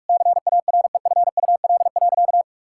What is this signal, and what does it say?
Morse code, heard as a keyed tone, about 43 wpm, 700 Hz. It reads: CAREFUL.
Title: Sentences from Top 1600 Words 45wpm